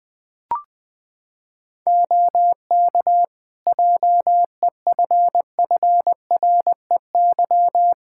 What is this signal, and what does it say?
Morse code, heard as a keyed tone, about 20 wpm, 700 Hz. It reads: OK JEFFREY
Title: Practice QSO - Exchange 4 - Ending - 20wpm